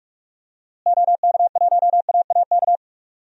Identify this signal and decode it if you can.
Morse code, heard as a keyed tone, about 41 wpm, 700 Hz. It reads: OK1AAK
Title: DXCC-100 Call Signs - 6 Characters - Repeated 2x - 45wpm